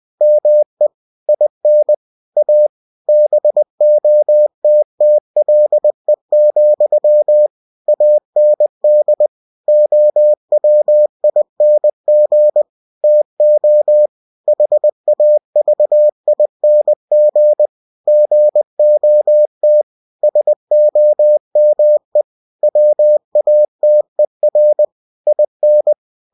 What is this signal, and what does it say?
Morse code, heard as a keyed tone, about 20 wpm, 600 Hz. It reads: ME IN A BOTTLE, AND OWING TO HAVING GOT SOME WATER IN
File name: cw-books / winnie-the-pooh / winnie-20wpm-page-0098